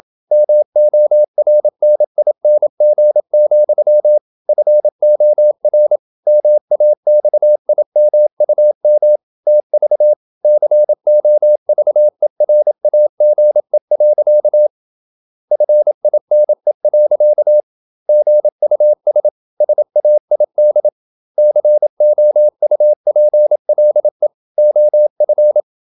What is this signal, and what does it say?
Morse code, heard as a keyed tone, about 27 wpm, 600 Hz. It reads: MORNING, FOR MAXIMUM TV COVERAGE. FINE. GUS SAID COUPLE OF
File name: Never 0027wpm